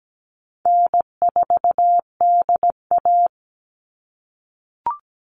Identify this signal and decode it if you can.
Morse code, heard as a keyed tone, about 17 wpm, 700 Hz. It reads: N4DA
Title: NA SOTA Chaser Callsigns 17wpm